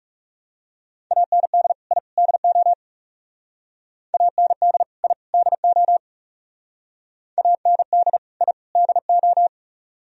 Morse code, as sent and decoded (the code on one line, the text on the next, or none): .- -. -.. / .. / -.. --- / .- -. -.. / .. / -.. --- / .- -. -.. / .. / -.. ---
AND I DO AND I DO AND I DO